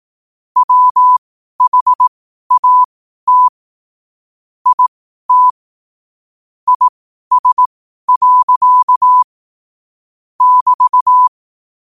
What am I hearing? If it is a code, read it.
Morse code, heard as a keyed tone, about 18 wpm, spaced as for 12 wpm, 1000 Hz. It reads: WHAT IT IS. <BT>
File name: qotdfiles / 12wpm